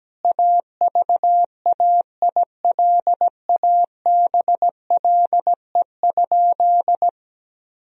Morse code, heard as a keyed tone, about 17 wpm, 700 Hz. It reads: AVAILABLE?
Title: Sentences from Top 1000 Words - Review Part 3 - 17wpm